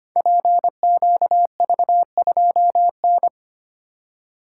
Morse code, heard as a keyed tone, about 25 wpm, 700 Hz. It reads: PQV2N
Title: Code Groups - 5 Letter:Numbers - 25wpm